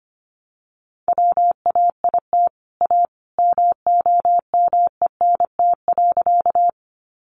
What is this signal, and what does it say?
Morse code, heard as a keyed tone, about 25 wpm, 700 Hz. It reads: WAIT A MOMENT.
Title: Sentences from Top 1000 Words 25wpm